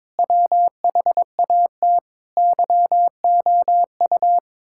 Morse code, heard as a keyed tone, about 22 wpm, 700 Hz. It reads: WHAT YOU
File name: Sentences with 9-Word Phrases - 22wpm